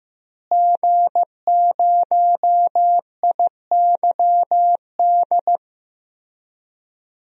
Morse code, heard as a keyed tone, about 15 wpm, 700 Hz. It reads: G0IYD
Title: DXCC-100 Call Signs - 5 Characters - Repeated 3x - 15wpm